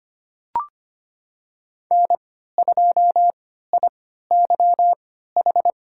Morse code, heard as a keyed tone, about 25 wpm, spaced as for 14 wpm, 700 Hz. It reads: N2IYH
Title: USA Call Signs - 5 Characters - 3x Character Spacing - 25wpm